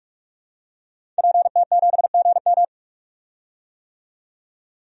Morse code, heard as a keyed tone, about 47 wpm, 700 Hz. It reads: PT7GM
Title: DXCC-100 Call Signs - 5 Characters - 45wpm